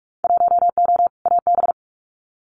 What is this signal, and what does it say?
Morse code, heard as a keyed tone, about 46 wpm, 700 Hz. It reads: 1O AB
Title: Winter Field Day Exchanges Encore 45wpm